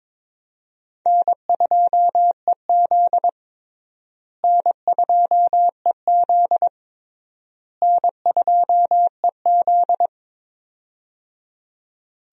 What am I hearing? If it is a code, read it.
Morse code, heard as a keyed tone, about 22 wpm, 700 Hz. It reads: N2EZ N2EZ N2EZ
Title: USA Call Signs - 4 Characters - Repeated 3x - 22wpm